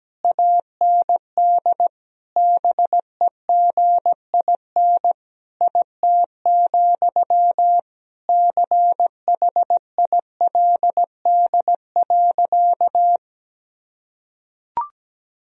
Morse code, heard as a keyed tone, about 17 wpm, 700 Hz. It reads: AND BEGIN IT, CHILD.
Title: Sentences from Top 1000 Words - Review Part 1 - 17wpm